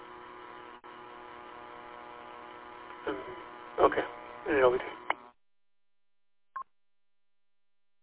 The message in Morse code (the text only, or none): none